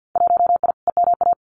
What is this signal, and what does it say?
Morse code, heard as a keyed tone, about 50 wpm, 700 Hz. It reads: 1I EMA
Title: Winter Field Day Exchanges Encore 50wpm